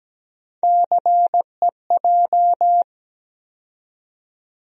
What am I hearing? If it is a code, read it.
Morse code, heard as a keyed tone, about 17 wpm, 700 Hz. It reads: CEJ
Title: USA Call Sign Postfixes - 3 Letters 17wpm